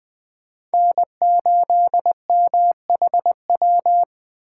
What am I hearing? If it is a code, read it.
Morse code, heard as a keyed tone, about 20 wpm, 700 Hz. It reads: N8MHW
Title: Speed Racing - USA 1x3 Call Signs 17wpm